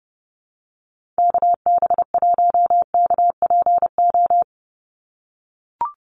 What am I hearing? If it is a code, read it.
Morse code, heard as a keyed tone, about 30 wpm, 700 Hz. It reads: KB1KPO
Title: USA Call Signs 30wpm